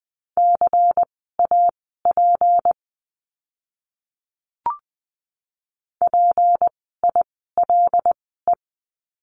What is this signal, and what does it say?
Morse code, heard as a keyed tone, about 20 wpm, spaced as for 15 wpm, 700 Hz. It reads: CAP PILE
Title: 153 Lesson 25f - Letter p - Words 20wpm